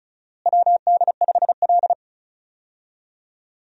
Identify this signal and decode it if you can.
Morse code, heard as a keyed tone, about 36 wpm, 700 Hz. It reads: WD5L
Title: USA Call Signs - 4 Characters - Repeated 2x - 35wpm